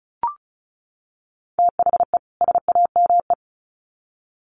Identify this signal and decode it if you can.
Morse code, heard as a keyed tone, about 34 wpm, 700 Hz. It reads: THE SAME
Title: Sets of 2 Words 35wpm